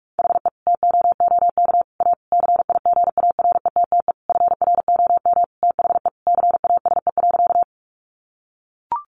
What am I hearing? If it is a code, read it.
Morse code, heard as a keyed tone, about 44 wpm, 700 Hz. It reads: HE TOOK A CIGARETTE FROM THE CASE.
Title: Sentences from Top 1000 Words 45wpm